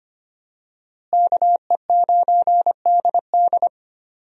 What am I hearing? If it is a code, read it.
Morse code, heard as a keyed tone, about 25 wpm, 700 Hz. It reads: KE9DD
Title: USA Call Signs - 5 Characters - Repeated 3x - 25wpm